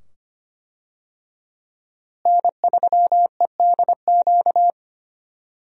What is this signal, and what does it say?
Morse code, heard as a keyed tone, about 25 wpm, 700 Hz. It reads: N3EDQ
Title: USA Call Signs - 5 Characters - Repeated 2x - 25wpm